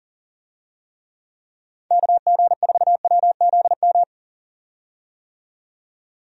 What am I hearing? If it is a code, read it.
Morse code, heard as a keyed tone, about 40 wpm, 700 Hz. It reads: KG4WZM
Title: USA Call Signs - 6 Characters - 40wpm